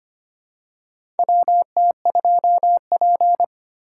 Morse code, heard as a keyed tone, about 25 wpm, 700 Hz. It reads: WT2P
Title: USA Call Signs - 3x Character Spacing - 25wpm